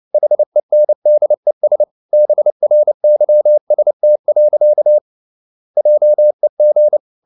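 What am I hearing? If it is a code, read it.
Morse code, heard as a keyed tone, about 29 wpm, 600 Hz. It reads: HENDES BRYST. JEG
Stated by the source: Fast (QRQ) Morse at 29 wpm, from CW med Gnister.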